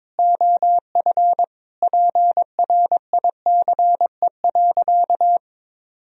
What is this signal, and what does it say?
Morse code, heard as a keyed tone, about 22 wpm, 700 Hz. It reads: OF PRICE.
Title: Sentences from Top 1000 Words - Review Part 4 - 22wpm